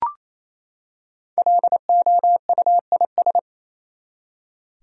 Morse code, heard as a keyed tone, about 28 wpm, 700 Hz. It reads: LOUIS
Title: USA Common Names - Max 5 Letters - 28wpm